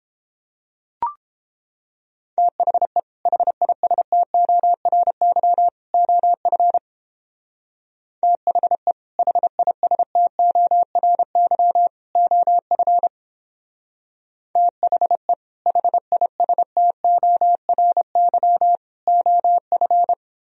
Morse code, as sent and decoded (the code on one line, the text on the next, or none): - .... / .... .. ... - --- .-. -.-- / --- ..-. / - .... / .... .. ... - --- .-. -.-- / --- ..-. / - .... / .... .. ... - --- .-. -.-- / --- ..-.
THE HISTORY OF THE HISTORY OF THE HISTORY OF